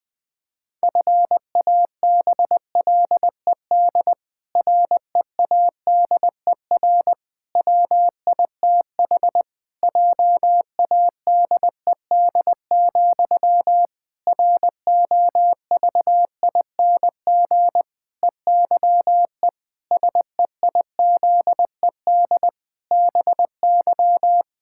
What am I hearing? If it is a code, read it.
Morse code, heard as a keyed tone, about 20 wpm, 700 Hz. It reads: FABLED READER WITH JADED, ROVING EYE SEIZED BY
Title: Pangram Challenge v2 - 20wpm